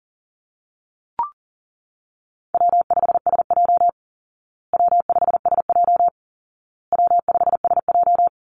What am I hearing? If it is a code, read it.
Morse code, heard as a keyed tone, about 37 wpm, 700 Hz. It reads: W5SJ W5SJ W5SJ
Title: Easy Call Signs Repeated 3x - 40wpm